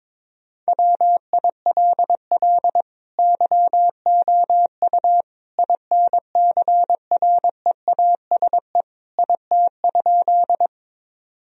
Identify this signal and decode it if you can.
Morse code, heard as a keyed tone, about 22 wpm, 700 Hz. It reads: WILL YOU INCREASE IT?
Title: Sentences from Top 1000 Words - Review Part 4 - 22wpm